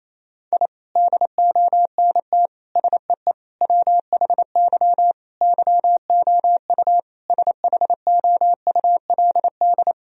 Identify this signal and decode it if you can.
Morse code, heard as a keyed tone, about 28 wpm, 700 Hz. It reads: I DONT SEE WHY YOU SHOULD
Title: Sentences from Top 1000 Words - Review Part 2 - 28wpm